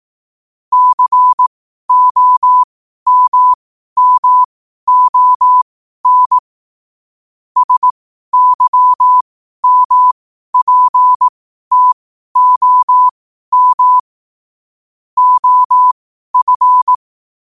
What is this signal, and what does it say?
Morse code, heard as a keyed tone, about 18 wpm, spaced as for 12 wpm, 1000 Hz. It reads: COMMON SYMPTOM OF
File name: qotdfiles / 12wpm